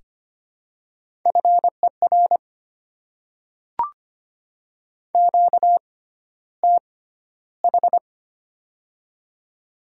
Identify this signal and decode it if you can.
Morse code, heard as a keyed tone, about 25 wpm, spaced as for 13 wpm, 700 Hz. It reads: FER QTH
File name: Top 20 QSO Elements - 6x Character Spacing - 25wpm